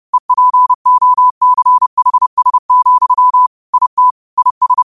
Morse code, heard as a keyed tone, about 30 wpm, 1000 Hz. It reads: EPOCHS, IT IS
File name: qotdfiles / 30wpm